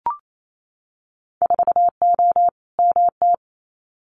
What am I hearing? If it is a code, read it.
Morse code, heard as a keyed tone, about 28 wpm, 690 Hz. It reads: 4O MT
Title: Winter Field Day Exchanges Encore 28wpm